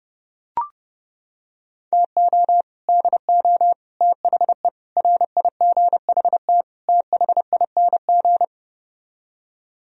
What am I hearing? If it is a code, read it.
Morse code, heard as a keyed tone, about 31 wpm, 700 Hz. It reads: TO DO THE RIGHT THING